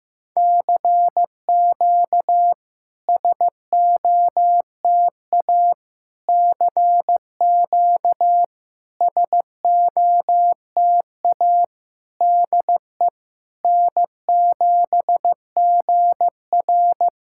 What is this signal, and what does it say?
Morse code, heard as a keyed tone, about 15 wpm, 700 Hz. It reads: CQ SOTA CQ SOTA DE N7GR
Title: SOTA QSO Practice 15wpm